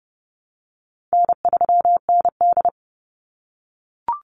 Morse code, heard as a keyed tone, about 31 wpm, 700 Hz. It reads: N3ND
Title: Easy Call Signs - 30wpm